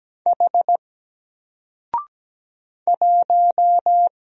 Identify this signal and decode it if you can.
Morse code, heard as a keyed tone, about 17 wpm, 700 Hz. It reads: H 1
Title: Single Character (Letter:Number:Punctuation:Prosign) 17wpm